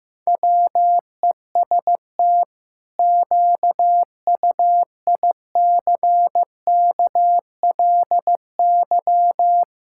Morse code, heard as a keyed tone, about 15 wpm, 700 Hz. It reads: WEST QUICKLY
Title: Pangram Challenge v2 - 15wpm